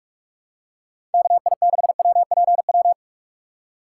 Morse code, heard as a keyed tone, about 43 wpm, 700 Hz. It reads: KI6WPW
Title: USA Call Signs - 6 Characters - 6x Character Spacing - 45wpm